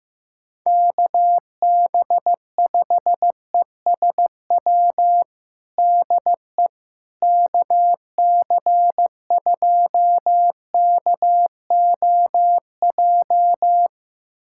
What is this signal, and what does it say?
Morse code, heard as a keyed tone, about 15 wpm, 700 Hz. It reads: KB5ESW DE KC2KOJ
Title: USA Callsign DE Callsign 15wpm